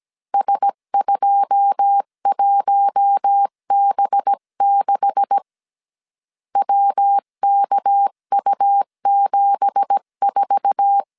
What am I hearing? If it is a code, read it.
Morse code, heard as a keyed tone, about 17 wpm, 800 Hz. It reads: S21B6 WKU74
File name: 1500C S B 16wpm